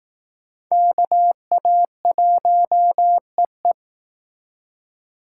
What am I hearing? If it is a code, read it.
Morse code, heard as a keyed tone, about 18 wpm, 700 Hz. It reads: KA1EE
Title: Speed Racing - USA 2x2 Call Signs 15wpm